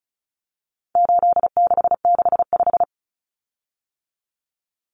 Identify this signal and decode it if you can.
Morse code, heard as a keyed tone, about 34 wpm, 700 Hz. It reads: 8665
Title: Common Cut Numbers - 4-Digit 35wpm